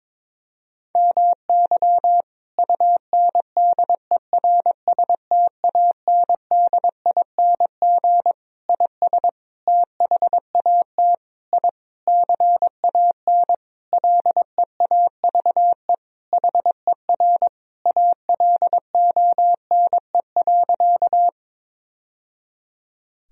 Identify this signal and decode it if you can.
Morse code, heard as a keyed tone, about 22 wpm, 700 Hz. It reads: MY UNDERSTANDING IS THAT I CAN LEAVE HER ALONE.
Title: Sentences from Top 1000 Words - Review Part 3 - 22wpm